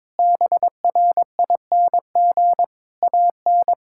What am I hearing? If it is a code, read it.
Morse code, heard as a keyed tone, about 21 wpm, 700 Hz. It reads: BRING AN
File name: practice / 22wpm / Sentences with 3-Word Phrases - 22wpm